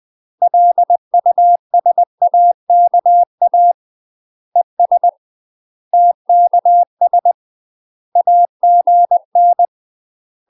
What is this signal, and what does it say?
Morse code, heard as a keyed tone, about 20 wpm, 700 Hz. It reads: LUSAKA ES TKS AGN
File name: UK MASTER QSOs TRAINING / 20WPM / UK MASTER QSOs 0035